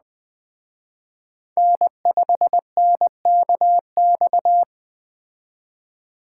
Morse code, heard as a keyed tone, about 20 wpm, 700 Hz. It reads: N5NKX
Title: Speed Racing - USA 1x3 Call Signs 17wpm